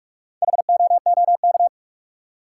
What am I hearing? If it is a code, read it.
Morse code, heard as a keyed tone, about 45 wpm, 700 Hz. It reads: HOOK
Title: Four Letter Words - 45wpm